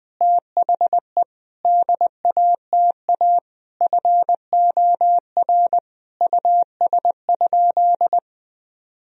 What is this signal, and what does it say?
Morse code, heard as a keyed tone, about 20 wpm, 700 Hz. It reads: THE DATA FOR US?
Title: Sentences from Top 1000 Words - Review Part 3 - 20wpm